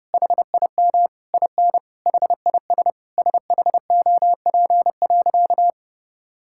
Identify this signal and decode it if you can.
Morse code, heard as a keyed tone, about 30 wpm, 700 Hz. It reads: HIM IN HIS SHOP.
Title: Sentences from Top 1600 Words 30wpm